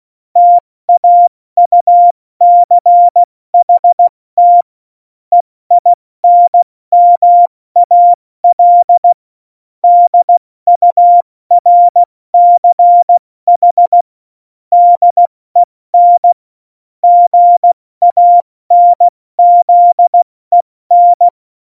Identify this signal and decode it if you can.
Morse code, heard as a keyed tone, about 16 wpm, 700 Hz. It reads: TAUCHT EINMAL DURCH DEN GANZEN
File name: Morsekurs / Witze / Witz 08. Dezember 2020, 70 F80